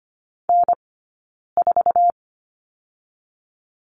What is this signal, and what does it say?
Morse code, heard as a keyed tone, about 25 wpm, spaced as for 9 wpm, 700 Hz. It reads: N4
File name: Lesson 15d - Number 5 - Two Characters 25wpm